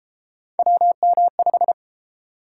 Morse code, heard as a keyed tone, about 33 wpm, 700 Hz. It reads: WM5